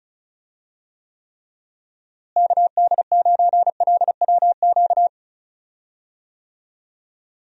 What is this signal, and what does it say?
Morse code, heard as a keyed tone, about 36 wpm, 700 Hz. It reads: KD9LWQ
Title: USA Call Signs - 6 Characters - Repeated 3x - 35wpm